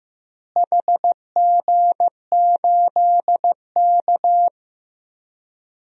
Morse code, heard as a keyed tone, about 16 wpm, 700 Hz. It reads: HG8K
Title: DXCC-100 Call Signs - 4 Characters - Repeated 3x - 15wpm